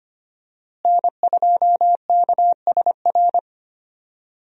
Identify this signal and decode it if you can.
Morse code, heard as a keyed tone, about 25 wpm, 700 Hz. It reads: N2KSR
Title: USA Call Signs - 5 Characters - Repeated 3x - 25wpm